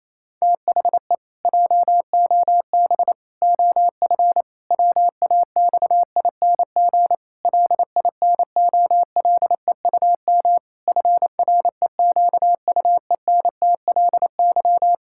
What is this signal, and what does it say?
Morse code, heard as a keyed tone, about 28 wpm, 700 Hz. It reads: THE JOB OF WAXING LINOLEUM FREQUENTLY
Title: Pangram Challenge v2 - 28wpm